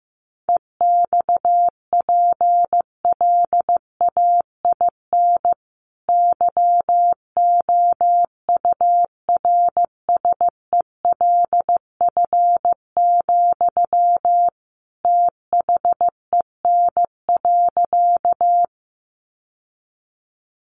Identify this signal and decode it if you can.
Morse code, heard as a keyed tone, about 15 wpm, 700 Hz. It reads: EXPLAIN YOURSELF, THEN.
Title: Sentences from Top 1100 Words 15wpm